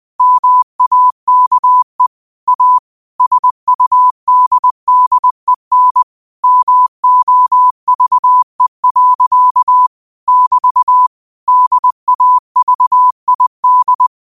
Morse code, heard as a keyed tone, about 20 wpm, 1000 Hz. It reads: MAKE A SUDDEN MOVE. <BT> DAVID